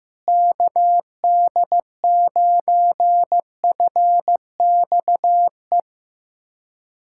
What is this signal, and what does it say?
Morse code, heard as a keyed tone, about 15 wpm, 700 Hz. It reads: KD9FXE
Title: SOTA QSO Practice 15wpm